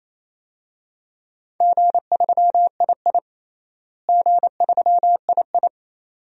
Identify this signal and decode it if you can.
Morse code, heard as a keyed tone, about 28 wpm, 700 Hz. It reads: G3II G3II
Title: DXCC-100 Call Signs - 4 Characters - Repeated 2x - 28wpm